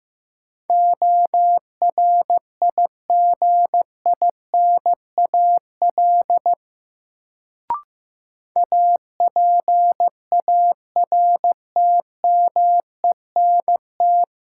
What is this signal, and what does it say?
Morse code, heard as a keyed tone, about 15 wpm, 700 Hz. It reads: ORIGINAL APARTMENT
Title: Sentences from Top 900 Words - New Word Drill - 15wpm